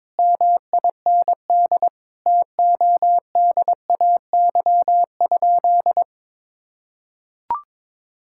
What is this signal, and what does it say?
Morse code, heard as a keyed tone, about 22 wpm, 700 Hz. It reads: MIND TODAY?
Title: Sentences from Top 1000 Words - Review Part 6 - 22wpm